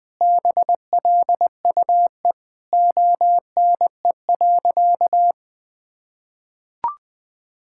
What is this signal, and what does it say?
Morse code, heard as a keyed tone, about 20 wpm, 700 Hz. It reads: BLUE ONE.
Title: Sentences with 2-Word Phrases - 20wpm